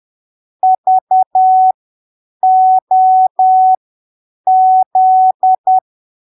Morse code, heard as a keyed tone, about 10 wpm, spaced as for 7 wpm, 750 Hz. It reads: VOZ
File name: arquivos / texto / texto 04 10ppm